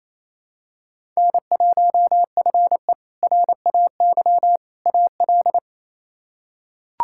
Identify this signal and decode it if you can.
Morse code, heard as a keyed tone, about 28 wpm, 700 Hz. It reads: N1FE RAY AL
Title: K1USN SST QSO Practice - 28wpm